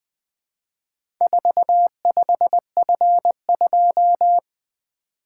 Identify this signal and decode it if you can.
Morse code, heard as a keyed tone, about 20 wpm, 700 Hz. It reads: 45F2
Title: Four Letter-Number Combinations - 20wpm